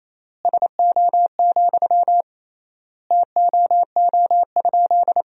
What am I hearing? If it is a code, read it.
Morse code, heard as a keyed tone, about 27 wpm, 700 Hz. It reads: SO, TOO?
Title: 3x Word-Spacing - Sentences from Top 200 Words - 28wpm